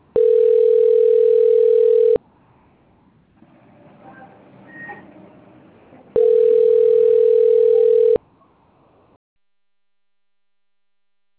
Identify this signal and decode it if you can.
Ringback tone (North America), 440 + 480 Hz.
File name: rg-1600-unknown- - - .87529